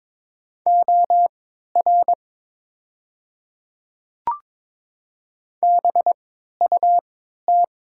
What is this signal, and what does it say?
Morse code, heard as a keyed tone, about 23 wpm, spaced as for 13 wpm, 700 Hz. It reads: OR BUT
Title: 3x Character Spacing - Top 100 Words - 22wpm